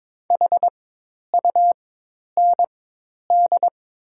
Morse code, heard as a keyed tone, about 22 wpm, spaced as for 10 wpm, 700 Hz. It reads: HUND
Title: 4x Character Spacing - Top 500 Words - 22wpm